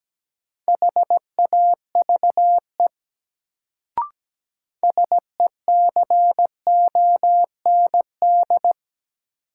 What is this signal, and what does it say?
Morse code, heard as a keyed tone, about 17 wpm, 700 Hz. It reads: HAVE SECOND
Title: Sets of 2 Words - New Word Drill - 17wpm